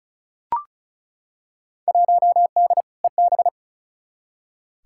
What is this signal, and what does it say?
Morse code, heard as a keyed tone, about 34 wpm, 700 Hz. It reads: 1D EB
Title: Field Day Exchanges - 35wpm